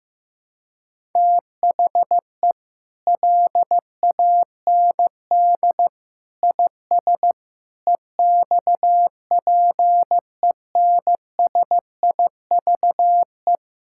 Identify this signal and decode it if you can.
Morse code, heard as a keyed tone, about 14 wpm, 700 Hz. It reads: THE LAND IS EXPENSIVE
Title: Sentences from Top 1000 Words - Review Part 6 - 15wpm